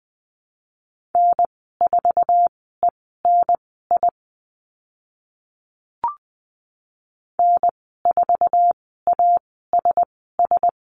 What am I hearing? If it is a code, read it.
Morse code, heard as a keyed tone, about 20 wpm, spaced as for 15 wpm, 700 Hz. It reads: N4ENI N4ASS
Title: 062 Lesson 11g - Letter h - Call Signs 20wpm